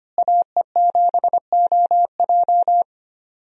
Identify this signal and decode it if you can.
Morse code, heard as a keyed tone, about 25 wpm, 700 Hz. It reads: AE7OJ
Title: USA Call Signs - 5 Characters - Repeated 3x - 25wpm